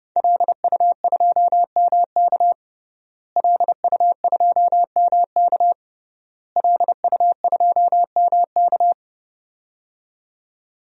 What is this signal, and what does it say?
Morse code, heard as a keyed tone, about 30 wpm, 700 Hz. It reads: LU2MK LU2MK LU2MK
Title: DXCC-100 Call Signs - 5 Characters - Repeated 3x - 30wpm